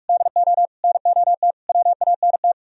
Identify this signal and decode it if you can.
Morse code, heard as a keyed tone, about 46 wpm, 700 Hz. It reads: DO NOT WANT